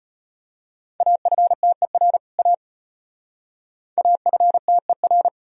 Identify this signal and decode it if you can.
Morse code, heard as a keyed tone, about 32 wpm, 700 Hz. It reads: AFTER A AFTER